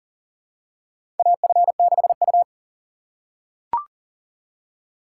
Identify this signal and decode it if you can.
Morse code, heard as a keyed tone, about 40 wpm, 700 Hz. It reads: AF6U
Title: USA Call Signs - 4 Characters - 3x Character Spacing - 40wpm